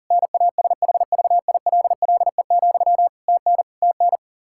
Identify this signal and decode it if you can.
Morse code, heard as a keyed tone, about 40 wpm, 700 Hz. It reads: NASHVILLE, TN TN
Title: Practice QSO - Exchange 1 - Beginning - 40wpm